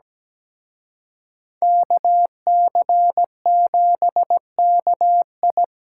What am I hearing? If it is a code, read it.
Morse code, heard as a keyed tone, about 20 wpm, 700 Hz. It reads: KC7KI
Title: Speed Racing - USA Call Signs 17wpm